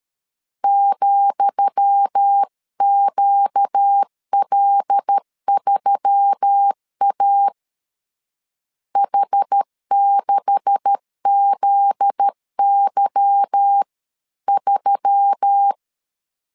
Morse code, heard as a keyed tone, about 13 wpm, 800 Hz. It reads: ,QL3A H6ZY3
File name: LEZIONI / 10 SIGNS / 10D COMA 12wpm 2544